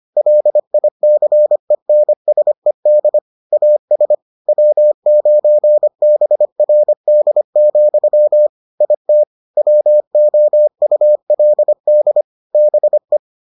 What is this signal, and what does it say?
Morse code, heard as a keyed tone, about 25 wpm, 600 Hz. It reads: LICENSED AS W9BRD, IT WOULD BE